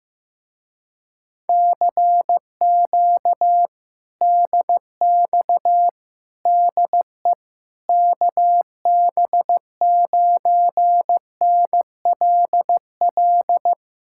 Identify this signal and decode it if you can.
Morse code, heard as a keyed tone, about 15 wpm, 700 Hz. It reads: CQ DX DE KB9NLL
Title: Speed-Racing Common QSO Phrases Encore - 15wpm